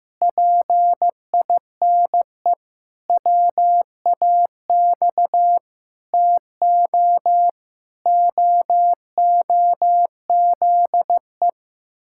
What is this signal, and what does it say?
Morse code, heard as a keyed tone, about 14 wpm, 700 Hz. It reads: PINE WAX TO OOZE
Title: Pangram Challenge v2 - 15wpm